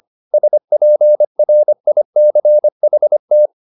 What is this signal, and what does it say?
Morse code, heard as a keyed tone, about 25 wpm, 600 Hz. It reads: SPRICHT